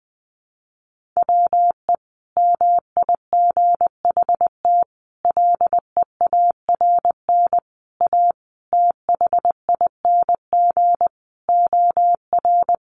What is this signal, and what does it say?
Morse code, heard as a keyed tone, about 20 wpm, 700 Hz. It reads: WE MIGHT LEARN A THING OR
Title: Sentences from Top 300 Words 20wpm